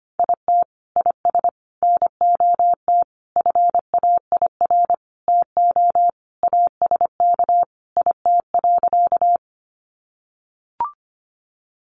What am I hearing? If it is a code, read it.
Morse code, heard as a keyed tone, about 25 wpm, 700 Hz. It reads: IT IS NOT FAIR TO ASK IT.
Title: Sentences from Top 1300 Words 25wpm